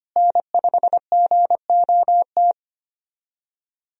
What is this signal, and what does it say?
Morse code, heard as a keyed tone, about 25 wpm, 700 Hz. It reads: N5GOT
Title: USA Call Signs - 5 Characters - Repeated 2x - 25wpm